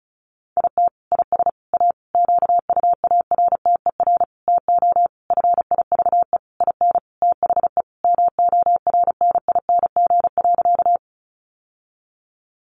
Morse code, heard as a keyed tone, about 34 wpm, 700 Hz. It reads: IT IS A QUARTER TO FIVE IN THE MORNING.
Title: Sentences from Top 1300 Words 35wpm